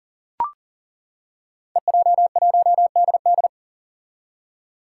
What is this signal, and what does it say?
Morse code, heard as a keyed tone, about 40 wpm, 700 Hz. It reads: EJ1DD
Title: Five Challenging Letter-Number Combinations 40wpm